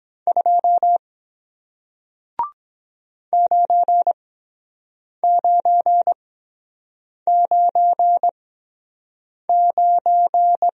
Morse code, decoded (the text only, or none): 2 9999